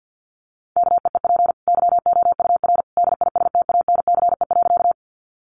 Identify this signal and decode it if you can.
Morse code, heard as a keyed tone, about 50 wpm, 700 Hz. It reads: KEEP YOUR DISTANCE.
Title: Sentences from Top 1200 Words 50wpm